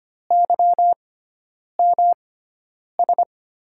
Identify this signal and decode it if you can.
Morse code, heard as a keyed tone, about 25 wpm, spaced as for 9 wpm, 700 Hz. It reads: YMS
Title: USA Call Signs - 5 Characters - 6x Character Spacing - 25wpm